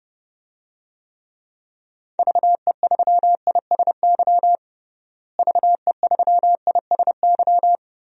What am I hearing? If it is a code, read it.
Morse code, heard as a keyed tone, about 30 wpm, 700 Hz. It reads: VE3ISY VE3ISY
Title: DXCC-100 Call Signs - 6 Characters - Repeated 2x - 30wpm